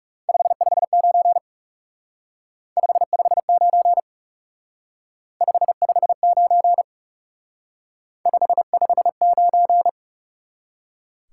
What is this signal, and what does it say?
Morse code, heard as a keyed tone, about 37 wpm, 700 Hz. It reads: 559 559 559 559
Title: Speed-Racing Top 90 QSO Elements 30wpm